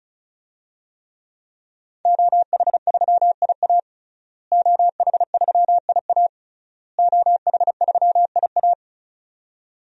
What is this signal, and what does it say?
Morse code, heard as a keyed tone, about 34 wpm, 700 Hz. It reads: OH3IA OH3IA OH3IA
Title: DXCC-100 Call Signs - 5 Characters - Repeated 3x - 35wpm